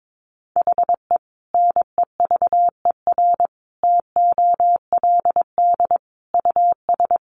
Morse code, heard as a keyed tone, about 22 wpm, 700 Hz. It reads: HE NEVER TOLD US
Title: Sentences from Top 400 Words 22wpm